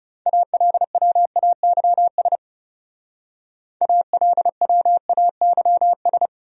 Morse code, heard as a keyed tone, about 36 wpm, 700 Hz. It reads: ALWAYS ALWAYS
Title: Speed-Racing Top 300 Words 30wpm